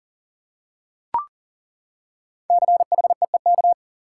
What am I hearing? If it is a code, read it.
Morse code, heard as a keyed tone, about 40 wpm, 700 Hz. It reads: CHEEK
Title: Five Letter Words - 40wpm